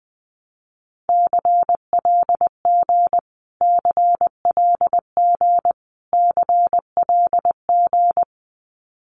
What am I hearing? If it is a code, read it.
Morse code, heard as a keyed tone, about 20 wpm, 700 Hz. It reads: CLG CLG CLG
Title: Abbreviations Repeated 3x - 20wpm